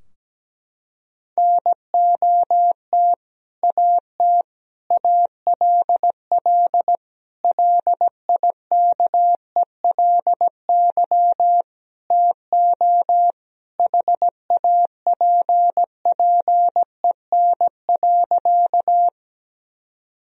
Morse code, heard as a keyed tone, about 17 wpm, 700 Hz. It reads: NOT AT ALL LIKELY TO HAPPEN.
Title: Sentences from Top 1000 Words - Review Part 4 - 17wpm